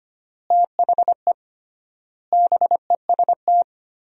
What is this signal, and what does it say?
Morse code, heard as a keyed tone, about 25 wpm, 700 Hz. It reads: THE BEST
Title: 3x Word-Spacing - Sentences from Top 400 Words - 25wpm